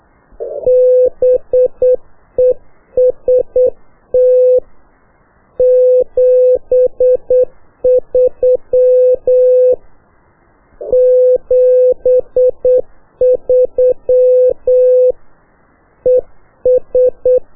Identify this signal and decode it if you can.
Morse code, heard as a keyed tone, about 8 wpm, 500 Hz. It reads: BEST 73 73 ES
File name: qso1-qrs